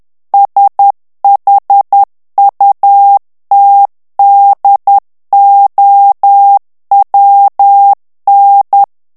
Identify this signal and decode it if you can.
Morse code, heard as a keyed tone, about 10 wpm, 800 Hz. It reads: SHUTDOWN
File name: ANGLAIS texte clair 3 vitesse 10 mots minute